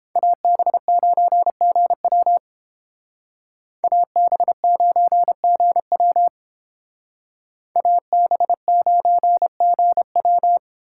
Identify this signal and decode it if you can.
Morse code, heard as a keyed tone, about 29 wpm, 700 Hz. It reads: AB9GW AB9GW AB9GW